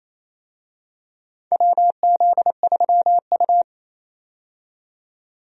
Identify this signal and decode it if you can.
Morse code, heard as a keyed tone, about 27 wpm, 700 Hz. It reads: WZ3U
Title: USA Call Signs - 4 Characters - 28wpm